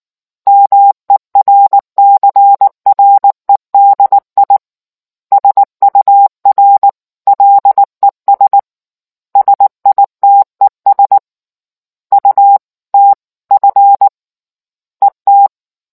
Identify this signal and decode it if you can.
Morse code, heard as a keyed tone, about 19 wpm, 800 Hz. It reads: MERCREDI SUR LES SITES UTF ET